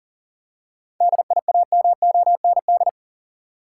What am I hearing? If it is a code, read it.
Morse code, heard as a keyed tone, about 39 wpm, 700 Hz. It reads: DIAMOND
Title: Seven Letter Words - 40wpm